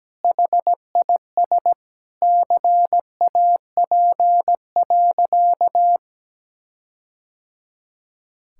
Morse code, heard as a keyed tone, about 17 wpm, 700 Hz. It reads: HIS CAP.
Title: Sentences from Top 1600 Words 17wpm